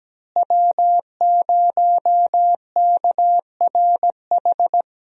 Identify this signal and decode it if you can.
Morse code, heard as a keyed tone, about 17 wpm, 700 Hz. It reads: W0KRH
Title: USA Call Signs - 5 Characters - Repeated 2x - 17wpm